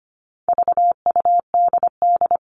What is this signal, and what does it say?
Morse code, heard as a keyed tone, about 25 wpm, 700 Hz. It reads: VUDD
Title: All Cut Numbers - 4-Digit 25wpm